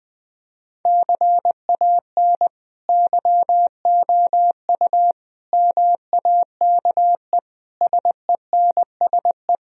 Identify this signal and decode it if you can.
Morse code, heard as a keyed tone, about 21 wpm, 700 Hz. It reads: CAN YOU MAKE SENSE
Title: Sentences from Top 1000 Words - Review Part 4 - 20wpm